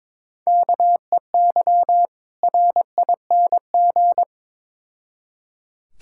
Morse code, heard as a keyed tone, about 22 wpm, 700 Hz. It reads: KEY RING
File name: Difficult Words with Hints - 22wpm